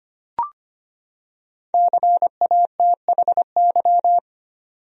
Morse code, heard as a keyed tone, about 25 wpm, 700 Hz. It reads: CATHY
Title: USA Common Names - 25wpm